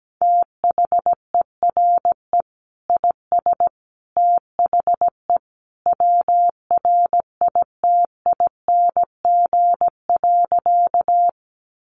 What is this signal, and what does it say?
Morse code, heard as a keyed tone, about 17 wpm, 700 Hz. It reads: THERE IS THE WRITING.
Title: Sentences from Top 800 Words 17wpm